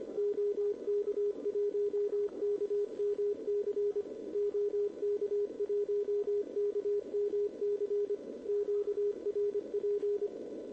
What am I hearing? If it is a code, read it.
Morse code, heard as a keyed tone, about 25 wpm, 410 Hz, with radio noise. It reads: OK1KMC OK1KMC QRP
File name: OK QRP 2026 / OKQRP2026